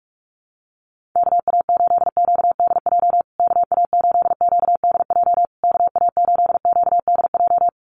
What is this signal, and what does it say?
Morse code, heard as a keyed tone, about 44 wpm, 700 Hz. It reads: KA8QDJ KA8QDJ KA8QDJ
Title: USA Call Signs Repeated 3x 45wpm